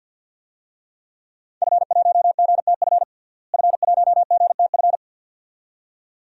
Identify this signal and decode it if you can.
Morse code, heard as a keyed tone, about 53 wpm, 700 Hz. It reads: F1GTF F1GTF
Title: DXCC-100 Call Signs - 5 Characters - Repeated 2x - 50wpm